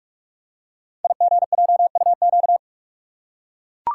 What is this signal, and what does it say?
Morse code, heard as a keyed tone, about 44 wpm, 700 Hz. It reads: IGJUQ
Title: Code Groups - 5 Letters - 45wpm